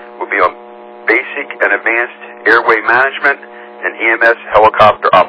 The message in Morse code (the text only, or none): none